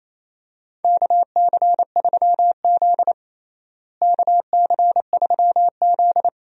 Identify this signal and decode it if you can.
Morse code, heard as a keyed tone, about 28 wpm, 700 Hz. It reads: KC3Z KC3Z
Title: USA Call Signs - 4 Characters - Repeated 3x - 28wpm